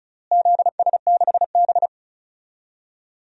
Morse code, heard as a keyed tone, about 36 wpm, 700 Hz. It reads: ZS6B
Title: DXCC-100 Call Signs - 4 Characters - Repeated 3x - 35wpm